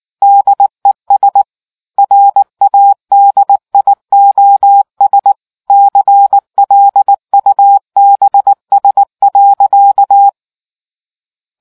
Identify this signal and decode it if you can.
Morse code, heard as a keyed tone, about 19 wpm, 800 Hz. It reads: DES RADIOS CLUBS.